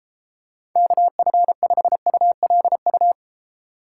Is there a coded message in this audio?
Yes: Morse code from 0.8 to 3.1 s, heard as a keyed tone, about 33 wpm, 700 Hz. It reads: KF5ULU